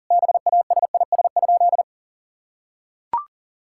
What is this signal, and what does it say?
Morse code, heard as a keyed tone, about 37 wpm, 700 Hz. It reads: BASIS?
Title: Sentences from Top 1000 Words - Review Part 5 - 40wpm